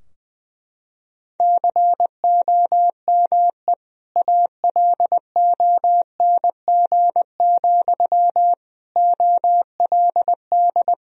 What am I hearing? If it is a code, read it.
Morse code, heard as a keyed tone, about 20 wpm, 700 Hz. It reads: COME ALONG, OLD
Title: Sentences from Top 1000 Words - Review Part 2 - 20wpm